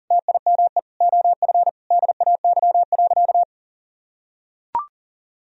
Morse code, heard as a keyed tone, about 37 wpm, 700 Hz. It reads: TIME OF DAY.
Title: Sentences from Top 1000 Words - Review Part 4 - 40wpm